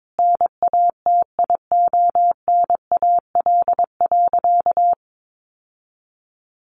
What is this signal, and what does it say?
Morse code, heard as a keyed tone, about 22 wpm, 700 Hz. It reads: NATIONAL.
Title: Sentences from Top 1200 Words 22wpm